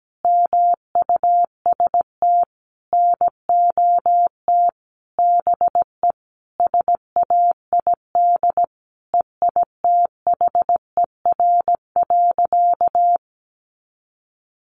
Morse code, heard as a keyed tone, about 17 wpm, 700 Hz. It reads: MUST NOT BE SAID EITHER.
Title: Sentences from Top 600 Words 17wpm